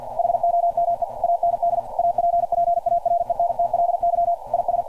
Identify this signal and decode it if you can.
Morse code, heard as a keyed tone, about 48 wpm, 690 Hz. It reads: HOME AND APPARENTLY IS
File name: archived / CRSnet